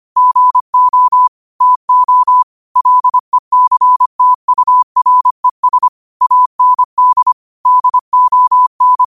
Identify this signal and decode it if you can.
Morse code, heard as a keyed tone, about 25 wpm, 1000 Hz. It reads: GO TO LECTURES AND DON
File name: qotdfiles / 25wpm